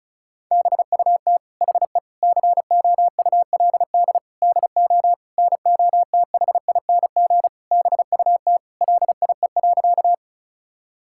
Morse code, heard as a keyed tone, about 36 wpm, 700 Hz. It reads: BUT HE COULD DO NOTHING BUT LIE.
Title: Sentences from Top 1600 Words 35wpm